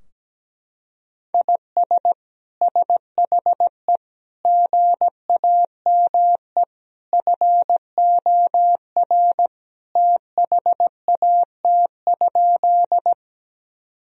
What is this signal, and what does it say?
Morse code, heard as a keyed tone, about 17 wpm, 700 Hz. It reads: IS SHE GAME FOR THAT?
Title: Sentences from Top 1000 Words - Review Part 3 - 17wpm